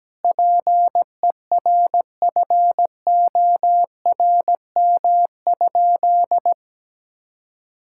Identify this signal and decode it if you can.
Morse code, heard as a keyed tone, about 17 wpm, 700 Hz. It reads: PERFORM?
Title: Sentences from Top 1000 Words - Review Part 6 - 17wpm